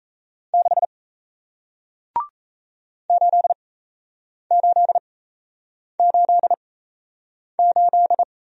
Morse code, heard as a keyed tone, about 36 wpm, spaced as for 8 wpm, 700 Hz. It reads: 6 8888